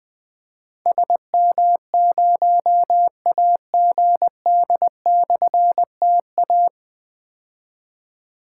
Morse code, heard as a keyed tone, about 21 wpm, 700 Hz. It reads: SM0AGD/TA
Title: Difficult Call Signs 20wpm